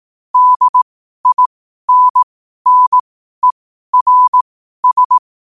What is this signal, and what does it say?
Morse code, heard as a keyed tone, about 18 wpm, spaced as for 12 wpm, 1000 Hz. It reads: DINNERS
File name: qotdfiles / 12wpm